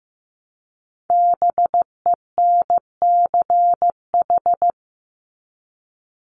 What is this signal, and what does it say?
Morse code, heard as a keyed tone, about 15 wpm, 700 Hz. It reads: BENCH
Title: Top 1251-1500 Words 15wpm